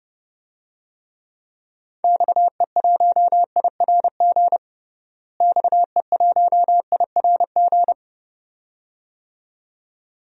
Morse code, heard as a keyed tone, about 32 wpm, 700 Hz. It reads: XE1IRG XE1IRG
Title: DXCC-100 Call Signs - 6 Characters - Repeated 2x - 30wpm